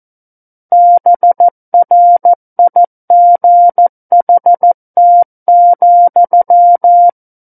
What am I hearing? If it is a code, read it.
Morse code, heard as a keyed tone, about 14 wpm, 700 Hz. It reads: BRIGHT,
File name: Book One / 05-Chapter 5 - 14 WPM